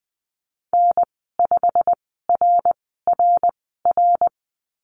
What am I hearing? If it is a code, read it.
Morse code, heard as a keyed tone, about 20 wpm, spaced as for 15 wpm, 700 Hz. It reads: N5RRR
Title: 173 Lesson 28g - Number 9 - Call Signs 20wpm